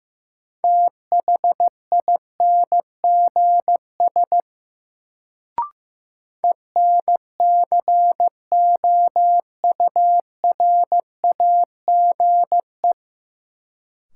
Morse code, heard as a keyed tone, about 15 wpm, 700 Hz. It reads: THINGS ENCOURAGE
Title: Sentences from Top 1400 Words - New Word Drill Alt - 15wpm